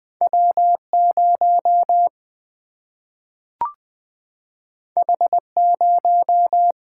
Morse code, heard as a keyed tone, about 20 wpm, 700 Hz. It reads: W0 H0
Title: Single-Letter Single-Number 20wpm